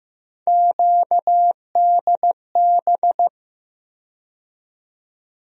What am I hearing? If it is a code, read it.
Morse code, heard as a keyed tone, about 15 wpm, 700 Hz. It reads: QDB
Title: USA Call Sign Postfixes - 3 Letters 15wpm